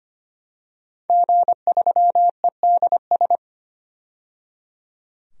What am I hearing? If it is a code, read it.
Morse code, heard as a keyed tone, about 25 wpm, 700 Hz. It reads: G3EDS
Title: DXCC-100 Call Signs - 5 Characters - 25wpm